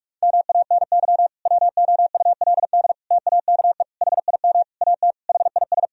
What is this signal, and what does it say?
Morse code, heard as a keyed tone, about 46 wpm, 700 Hz. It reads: MANY WOULD TAKE HIM AT HIS
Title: Sentences from Top 1000 Words - Review Part 1 - 45wpm